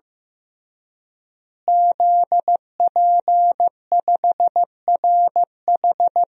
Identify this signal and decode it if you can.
Morse code, heard as a keyed tone, about 15 wpm, 700 Hz. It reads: ZP5RH
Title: DXCC-100 Call Signs - 5 Characters - Repeated 3x - 15wpm